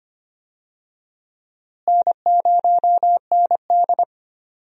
Morse code, heard as a keyed tone, about 25 wpm, 700 Hz. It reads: N0ND
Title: USA Call Signs - 4 Characters - Repeated 3x - 25wpm